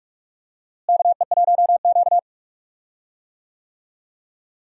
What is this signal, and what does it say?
Morse code, heard as a keyed tone, about 47 wpm, 700 Hz. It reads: KE1Q
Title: USA Call Signs - 4 Characters - Repeated 2x - 45wpm